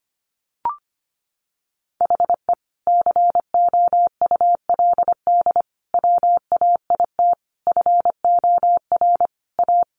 Morse code, heard as a keyed tone, about 25 wpm, 700 Hz. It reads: HE COULD WAIT FOR A
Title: Sentences from Top 1300 Words 25wpm